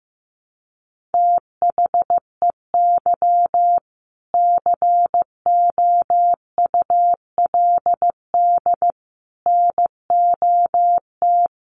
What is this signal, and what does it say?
Morse code, heard as a keyed tone, about 15 wpm, 700 Hz. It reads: THEY COULD NOT
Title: Sentences from Top 1000 Words 15wpm